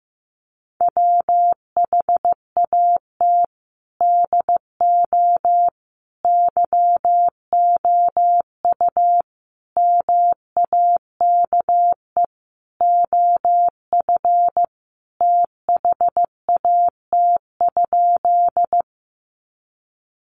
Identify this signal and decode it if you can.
Morse code, heard as a keyed tone, about 15 wpm, 700 Hz. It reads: WHAT DO YOU MAKE OF THAT?
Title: Sentences from Top 100 Words 15wpm